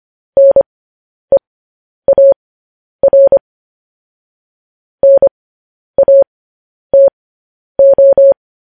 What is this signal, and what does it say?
Morse code, heard as a keyed tone, about 25 wpm, spaced as for 10 wpm, 550 Hz. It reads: NEAR NATO
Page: News Headlines in Morse Code at 10 WPM